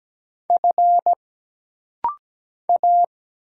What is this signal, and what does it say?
Morse code, heard as a keyed tone, about 17 wpm, 700 Hz. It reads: F A
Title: Single Letter-Number Alt - Rapid-Fire 17wpm